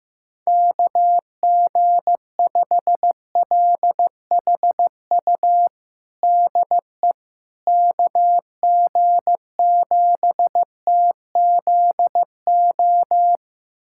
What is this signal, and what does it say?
Morse code, heard as a keyed tone, about 15 wpm, 700 Hz. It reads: KG5LHU DE KG7TZO
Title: USA Callsign DE Callsign - 6 Characters - 15wpm